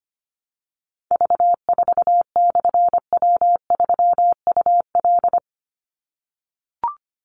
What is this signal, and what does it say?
Morse code, heard as a keyed tone, about 25 wpm, 700 Hz. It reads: V4/W3UL
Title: Call Signs - 25wpm